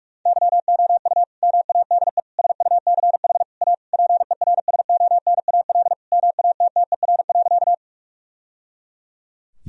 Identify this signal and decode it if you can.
Morse code, heard as a keyed tone, about 46 wpm, 700 Hz. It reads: YOU MADE SUCH A PERSONAL MATTER.
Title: Sentences from Top 1000 Words - Review Part 4 - 45wpm